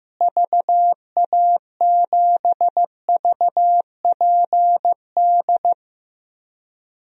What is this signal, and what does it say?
Morse code, heard as a keyed tone, about 16 wpm, 700 Hz. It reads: VA7VPD
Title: DXCC-100 Call Signs - 6 Characters - Repeated 3x - 15wpm